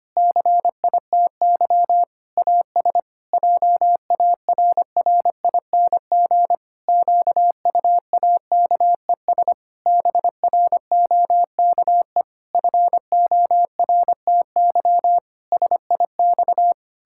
Morse code, heard as a keyed tone, about 25 wpm, 700 Hz. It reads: CITY AS JARRING QUAKES BROKE FORTY SIX
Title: Pangram Challenge v2 - 25wpm